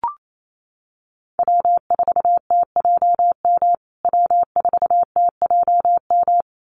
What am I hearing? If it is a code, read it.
Morse code, heard as a keyed tone, about 28 wpm, 700 Hz. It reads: W4TJM W4TJM
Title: USA Call Signs Repeated 2x 28wpm